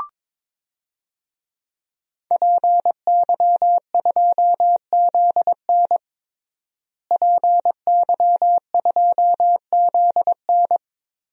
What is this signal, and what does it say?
Morse code, heard as a keyed tone, about 22 wpm, 700 Hz. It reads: PY2ZN PY2ZN
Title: DXCC-100 Call Signs - 5 Characters - Repeated 3x - 22wpm